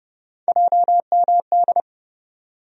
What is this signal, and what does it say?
Morse code, heard as a keyed tone, about 32 wpm, 700 Hz. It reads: JMD
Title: USA Call Sign Postfixes - 3 Letters 30wpm